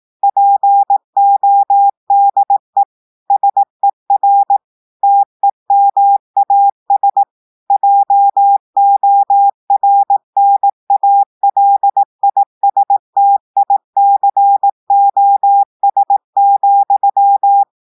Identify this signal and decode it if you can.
Morse code, heard as a keyed tone, about 18 wpm, 800 Hz. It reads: PODE SER TEMAS JORNALISTICOS,
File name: audios / portugais / portugais texte en clair 2 vitesse 18 mots minutes